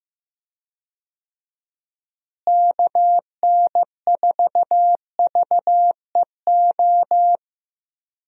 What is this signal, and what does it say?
Morse code, heard as a keyed tone, about 16 wpm, 700 Hz. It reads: KN4VEO
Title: USA Call Signs - 6 Characters - Repeated 3x - 15wpm